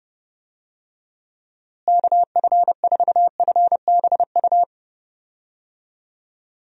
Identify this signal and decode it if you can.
Morse code, heard as a keyed tone, about 30 wpm, 700 Hz. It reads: KF4FBU
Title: USA Call Signs - 6 Characters - Repeated 2x - 30wpm